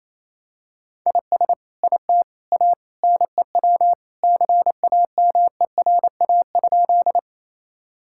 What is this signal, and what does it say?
Morse code, heard as a keyed tone, about 28 wpm, 700 Hz. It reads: IS IT A NEW CAMERA?
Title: Sentences from Top 1000 Words - Review Part 5 - 28wpm